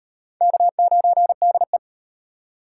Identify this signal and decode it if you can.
Morse code, heard as a keyed tone, about 38 wpm, 700 Hz. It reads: K9DE